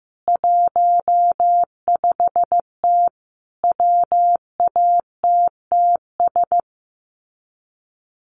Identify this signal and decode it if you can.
Morse code, heard as a keyed tone, about 15 wpm, 700 Hz. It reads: 15T WATTS
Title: Common QSO Phrases 15wpm